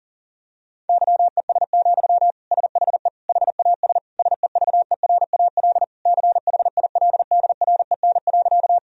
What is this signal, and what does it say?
Morse code, heard as a keyed tone, about 40 wpm, 700 Hz. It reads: YES, SHE HAS SEVERAL CHILDREN.
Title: Sentences from Top 1000 Words - Review Part 2 - 40wpm